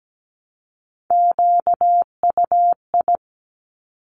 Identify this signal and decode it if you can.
Morse code, heard as a keyed tone, about 17 wpm, 700 Hz. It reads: QUI
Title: Top Three Letter Combinations in Words 17wpm